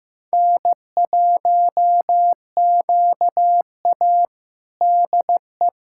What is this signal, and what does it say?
Morse code, heard as a keyed tone, about 16 wpm, 700 Hz. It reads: N1QA DE
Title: USA Callsign DE Callsign - 4 Characters - 15wpm